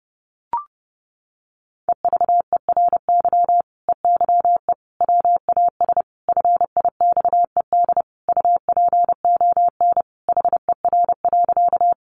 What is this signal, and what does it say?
Morse code, heard as a keyed tone, about 30 wpm, 700 Hz. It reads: EVERY EYE WAS FIXED UPON HER.
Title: Sentences from Top 1100 Words 30wpm